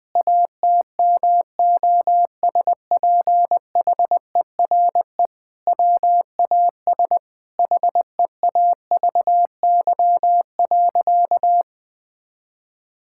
Morse code, heard as a keyed tone, about 20 wpm, 700 Hz. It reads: ATMOSPHERE WAS HEAVY.
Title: Sentences from Top 1600 Words 20wpm